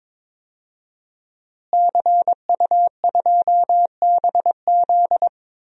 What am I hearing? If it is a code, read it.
Morse code, heard as a keyed tone, about 22 wpm, 700 Hz. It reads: CU2BZ
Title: DXCC-100 Call Signs - 5 Characters - Repeated 2x - 22wpm